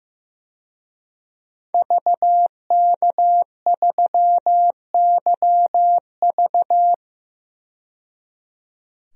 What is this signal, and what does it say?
Morse code, heard as a keyed tone, about 15 wpm, 700 Hz. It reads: VK3YV
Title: DXCC-100 Call Signs - 5 Characters - 15wpm